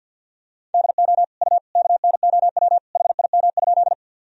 Morse code, heard as a keyed tone, about 50 wpm, 700 Hz. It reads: DO U KNOW HIM?
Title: Sentences from Top 100 Words and Abbreviations 50wpm